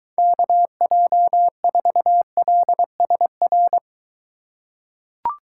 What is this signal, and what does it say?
Morse code, heard as a keyed tone, about 23 wpm, 700 Hz. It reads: KJ4LSR